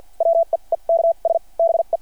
Morse code, heard as a keyed tone, about 53 wpm, spaced as for 39 wpm, 690 Hz. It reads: WEEKS BE
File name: archived / CRSnet